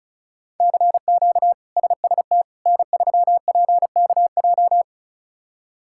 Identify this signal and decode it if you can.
Morse code, heard as a keyed tone, about 36 wpm, 700 Hz. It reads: CQ SST N3PKJ
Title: K1USN SST QSO Practice - 35wpm